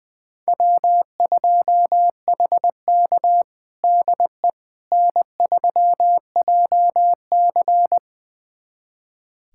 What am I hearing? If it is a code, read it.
Morse code, heard as a keyed tone, about 20 wpm, 700 Hz. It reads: W2HK DE N3JC
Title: USA Callsign DE Callsign - 4 Characters - 20wpm